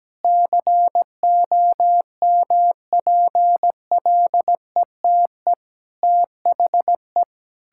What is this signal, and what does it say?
Morse code, heard as a keyed tone, about 17 wpm, 700 Hz. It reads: COMPLETE THE
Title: Sentences from Top 1000 Words - Review Part 6 - 17wpm